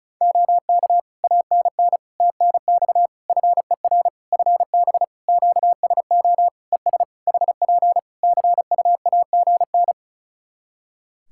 Morse code, heard as a keyed tone, about 34 wpm, 700 Hz. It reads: OK ANN TNX FER FB QSO ES HP CUAGN
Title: K1USN SST QSO Practice 35wpm